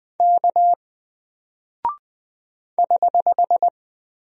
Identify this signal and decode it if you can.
Morse code, heard as a keyed tone, about 20 wpm, 700 Hz. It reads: K <HH>
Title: Single Character (Letter:Number:Punctuation:Prosign) 20wpm